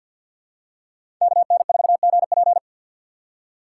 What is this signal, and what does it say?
Morse code, heard as a keyed tone, about 53 wpm, 700 Hz. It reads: KN4GP
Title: USA Call Signs - 5 Characters - 2x Character Spacing - 50wpm